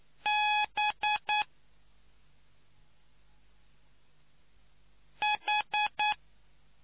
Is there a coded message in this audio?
Yes: Morse code from 0.3 to 6.1 s, heard as a keyed tone, about 9 wpm, 840 Hz. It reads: B H